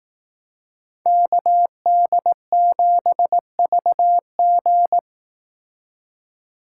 Morse code, heard as a keyed tone, about 18 wpm, 700 Hz. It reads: KD7VG
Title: Speed Racing - USA 2x2 Call Signs 15wpm